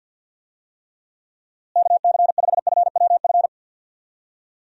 Morse code, heard as a keyed tone, about 50 wpm, 700 Hz. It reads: KC5FWF
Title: USA Call Signs - 6 Characters - Repeated 2x - 50wpm